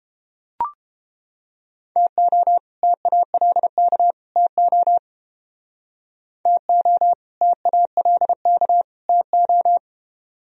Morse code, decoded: TO TALK TO TO TALK TO

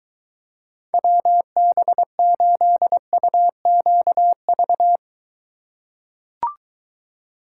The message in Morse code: WB8UQV

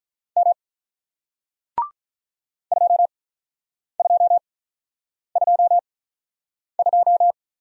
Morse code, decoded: M 2222

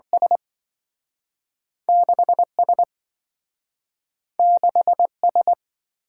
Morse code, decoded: S 6S 6S

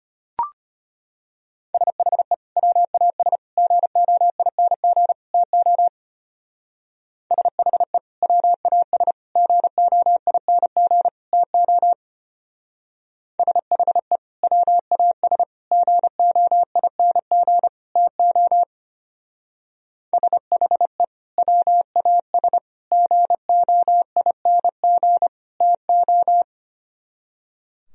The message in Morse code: SHE WAS GOING TO SHE WAS GOING TO SHE WAS GOING TO SHE WAS GOING TO

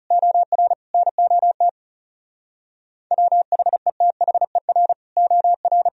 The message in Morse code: OR NOT WHETHER OR

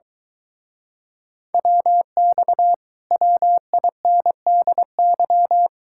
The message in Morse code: WX WINDY

